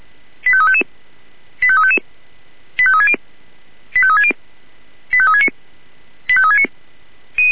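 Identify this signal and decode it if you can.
Five-tone selective call (ZVEI), heard as quick runs of five tones, tones of about 70 ms. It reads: 85355 85360 85360 85369 85369
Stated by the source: ZVEI five-tone selective call received on SDR.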